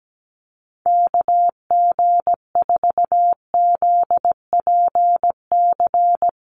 Morse code, heard as a keyed tone, about 17 wpm, 700 Hz. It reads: KG4ZPC
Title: USA Call Signs Repeated 2x 17wpm